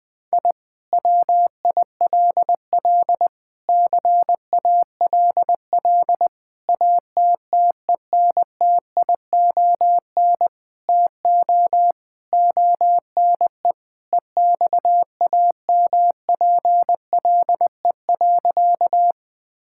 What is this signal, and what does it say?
Morse code, heard as a keyed tone, about 19 wpm, 700 Hz. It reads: I WILL CALL ATTENTION TO ONE EXAMPLE.
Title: Sentences from Top 1000 Words - Review Part 6 - 20wpm